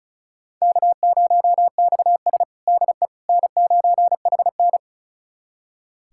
Morse code, heard as a keyed tone, about 36 wpm, 700 Hz. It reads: K0XS DE N9HN
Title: USA Callsign DE Callsign - 4 Characters - 35wpm